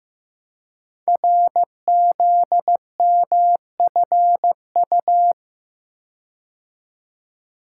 Morse code, heard as a keyed tone, about 15 wpm, 700 Hz. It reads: RZMFU
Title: Code Groups - 5 Letter:Numbers - 15wpm